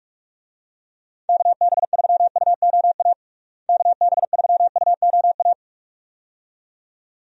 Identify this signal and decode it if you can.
Morse code, heard as a keyed tone, about 46 wpm, 700 Hz. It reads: KB3UOA KB3UOA
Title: USA Call Signs - 6 Characters - Repeated 2x - 45wpm